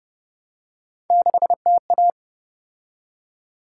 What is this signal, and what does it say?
Morse code, heard as a keyed tone, about 33 wpm, 700 Hz. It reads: 6TA